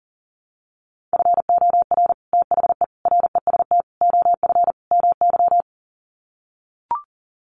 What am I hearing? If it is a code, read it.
Morse code, heard as a keyed tone, about 40 wpm, 700 Hz. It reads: FOR THE REST OF MY
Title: Sets of 5 Words 40wpm